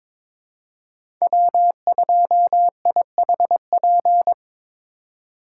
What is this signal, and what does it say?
Morse code, heard as a keyed tone, about 21 wpm, 700 Hz. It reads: W2IHP
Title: USA Call Signs - 2x Character Spacing - 22wpm